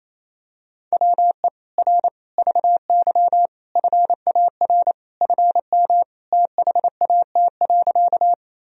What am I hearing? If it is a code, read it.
Morse code, heard as a keyed tone, about 28 wpm, 700 Hz. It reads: WE R VY FAR FM THAT.
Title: Sentences from Top 300 Words and Abbreviations 28wpm